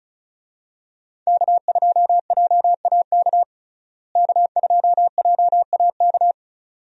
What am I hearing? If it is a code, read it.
Morse code, heard as a keyed tone, about 36 wpm, 700 Hz. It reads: K2JAK K2JAK
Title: USA Call Signs - 5 Characters - Repeated 2x - 35wpm